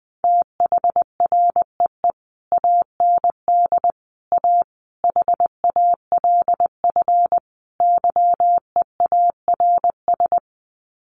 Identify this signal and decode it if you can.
Morse code, heard as a keyed tone, about 20 wpm, 700 Hz. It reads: THREE AND A HALF YEARS
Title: Sets of 5 Words Encore2 20wpm